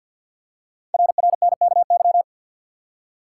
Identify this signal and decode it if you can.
Morse code, heard as a keyed tone, about 50 wpm, 700 Hz. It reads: RRNKY
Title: Code Groups - 5 Letters - 50wpm